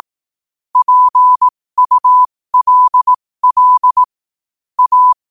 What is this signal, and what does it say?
Morse code, heard as a keyed tone, about 18 wpm, spaced as for 15 wpm, 1000 Hz. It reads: PULL A
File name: qotdfiles / 15wpm